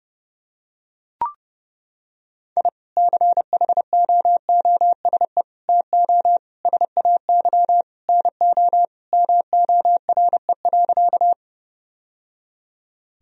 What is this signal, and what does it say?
Morse code, heard as a keyed tone, about 30 wpm, 700 Hz. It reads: I CHOOSE TO SAY NO MORE.
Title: Sentences from Top 1000 Words - Review Part 4 - 30wpm